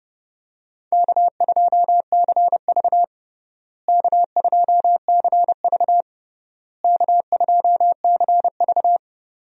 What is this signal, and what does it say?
Morse code, heard as a keyed tone, about 30 wpm, 700 Hz. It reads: K2CV K2CV K2CV
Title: USA Call Signs - 4 Characters - Repeated 3x - 30wpm